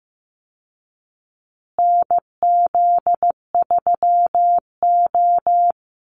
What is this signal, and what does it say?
Morse code, heard as a keyed tone, about 15 wpm, 700 Hz. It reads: NZ3O
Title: Call Signs Repeated 2x - 15wpm